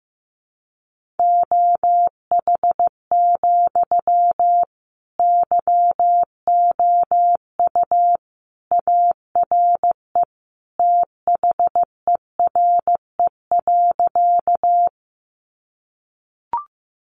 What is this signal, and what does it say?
Morse code, heard as a keyed tone, about 15 wpm, 700 Hz. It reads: OH, YOU ARE THERE.
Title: Sentences from Top 500 Words 15wpm